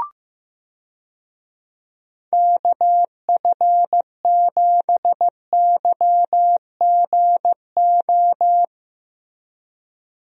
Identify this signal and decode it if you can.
Morse code, heard as a keyed tone, about 15 wpm, 700 Hz. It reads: KF7YGO
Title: USA Call Signs - 6 Characters - Repeated 2x - 15wpm